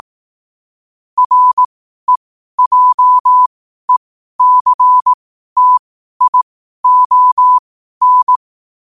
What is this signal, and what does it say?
Morse code, heard as a keyed tone, about 18 wpm, spaced as for 12 wpm, 1000 Hz. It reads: REJECTION
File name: qotdfiles / 12wpm